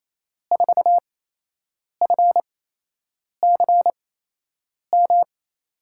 Morse code, heard as a keyed tone, about 27 wpm, spaced as for 8 wpm, 700 Hz. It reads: 4FCM
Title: USA Call Signs - 6 Characters - 8x Character Spacing - 28wpm